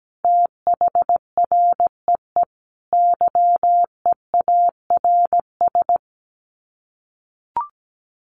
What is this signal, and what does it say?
Morse code, heard as a keyed tone, about 17 wpm, 700 Hz. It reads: THREE YEARS
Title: Sets of 5 Words Encore2 17wpm